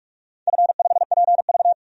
Morse code, heard as a keyed tone, about 41 wpm, 690 Hz. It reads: F5PV
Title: DXCC-100 Call Signs - 4 Characters - 45wpm